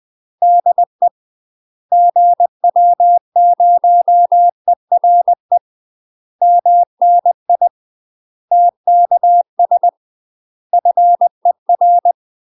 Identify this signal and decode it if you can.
Morse code, heard as a keyed tone, about 20 wpm, 700 Hz. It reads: DE GW0ERE MNI TKS FER
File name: UK MASTER QSOs TRAINING / 20WPM / UK MASTER QSOs 0008